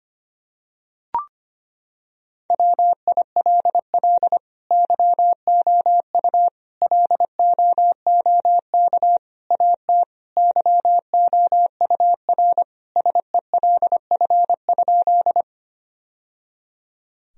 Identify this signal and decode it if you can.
Morse code, heard as a keyed tone, about 25 wpm, 700 Hz. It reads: WILL YOU LOOK AT YOUR SELF?
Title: Sentences from Top 1000 Words - Review Part 1 - 25wpm